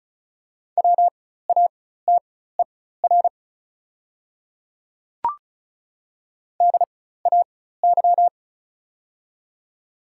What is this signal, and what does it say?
Morse code, heard as a keyed tone, about 34 wpm, spaced as for 16 wpm, 700 Hz. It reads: WATER DAY
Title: 4x Character Spacing - Top 100 Words - 35wpm